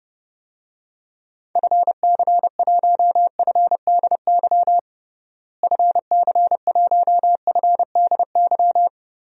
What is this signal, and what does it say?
Morse code, heard as a keyed tone, about 30 wpm, 700 Hz. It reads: FC1FDY FC1FDY
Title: DXCC-100 Call Signs - 6 Characters - Repeated 3x - 30wpm